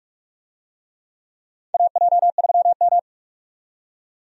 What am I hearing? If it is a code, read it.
Morse code, heard as a keyed tone, about 46 wpm, 700 Hz. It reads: AJ3M
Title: USA Call Signs - 4 Characters - 45wpm